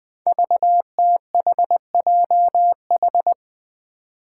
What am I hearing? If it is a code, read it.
Morse code, heard as a keyed tone, about 20 wpm, 700 Hz. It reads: VTHJH
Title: Five Challenging Letter-Number Combinations 20wpm